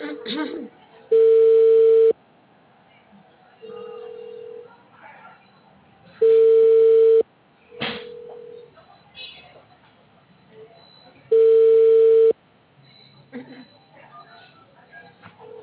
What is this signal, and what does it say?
Ringback tone (China, 450 Hz), with speech.